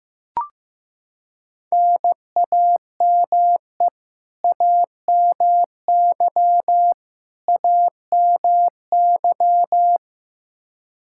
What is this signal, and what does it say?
Morse code, heard as a keyed tone, about 15 wpm, 700 Hz. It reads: NAME AMY AMY
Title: Practice QSO - Exchange 1 - Beginning - 15wpm